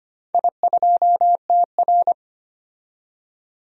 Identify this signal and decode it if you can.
Morse code, heard as a keyed tone, about 25 wpm, 700 Hz. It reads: I2TR
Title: DXCC-100 Call Signs - 4 Characters - 25wpm